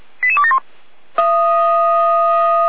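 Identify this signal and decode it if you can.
Five-tone selective call (ZVEI), heard as quick runs of five tones, tones of about 70 ms. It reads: 88271
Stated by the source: ZVEI five-tone call 88150 received on an SDR.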